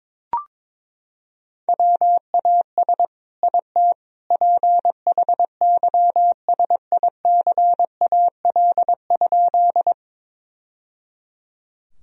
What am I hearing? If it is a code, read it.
Morse code, heard as a keyed tone, about 22 wpm, 700 Hz. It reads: WAS IT PHYSICAL?
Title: Sentences from Top 1000 Words - Review Part 4 - 22wpm